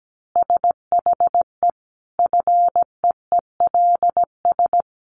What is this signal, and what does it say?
Morse code, heard as a keyed tone, about 17 wpm, 700 Hz. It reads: SHE FEELS
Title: Additional Easy Phrases 17wpm